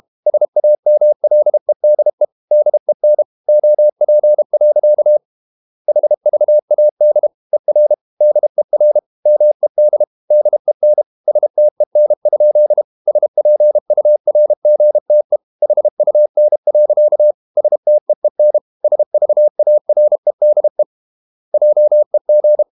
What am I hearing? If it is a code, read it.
Morse code, heard as a keyed tone, about 32 wpm, 600 Hz. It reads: SAMLEDE DEN OP. HVAD ER DER MED DEN STEN? SPURGTE HUN. STEEN SVAREDE JEG
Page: Ferie 32 wpm